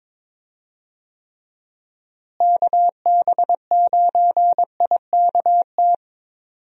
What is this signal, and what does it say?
Morse code, heard as a keyed tone, about 22 wpm, 700 Hz. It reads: KB9IKT
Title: USA Call Signs - 6 Characters - Repeated 3x - 22wpm